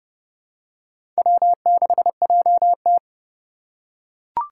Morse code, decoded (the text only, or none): W6JT